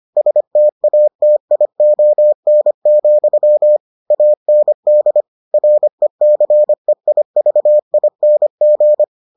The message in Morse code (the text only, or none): STATION, AND RECEIVING